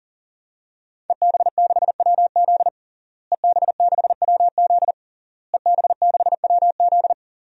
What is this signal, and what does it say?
Morse code, heard as a keyed tone, about 37 wpm, 700 Hz. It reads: EB6WZ EB6WZ EB6WZ
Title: DXCC-100 Call Signs - 5 Characters - Repeated 3x - 40wpm